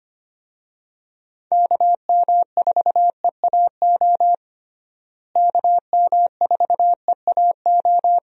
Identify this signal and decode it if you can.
Morse code, heard as a keyed tone, about 25 wpm, 700 Hz. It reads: KM4EAO KM4EAO
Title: USA Call Signs - 6 Characters - Repeated 3x - 25wpm